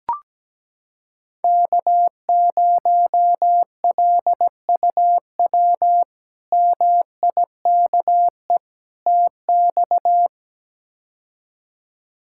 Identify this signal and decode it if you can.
Morse code, heard as a keyed tone, about 17 wpm, 700 Hz. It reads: K0LUW MIKE TX
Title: K1USN SST QSO Practice - 17wpm